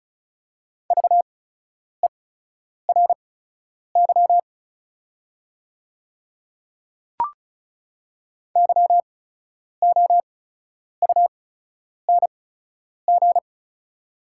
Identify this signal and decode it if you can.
Morse code, heard as a keyed tone, about 36 wpm, spaced as for 10 wpm, 700 Hz. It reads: VERY YOUNG
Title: 8x Character Spacing - Top 500 Words - 35wpm